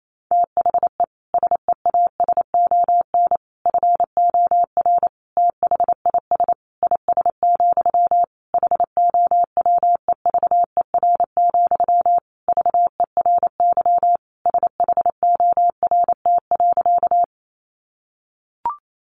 Morse code, heard as a keyed tone, about 28 wpm, 700 Hz. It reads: THE SEASON FOR THIS IS, HOWEVER, VERY SHORT.
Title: Sentences from Top 1400 Words 28wpm